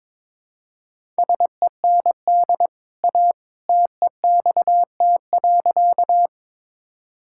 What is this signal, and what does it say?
Morse code, heard as a keyed tone, about 22 wpm, 700 Hz. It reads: SEND A TEXT.
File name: practice / 22wpm / Easy Sentences with Hints - 22wpm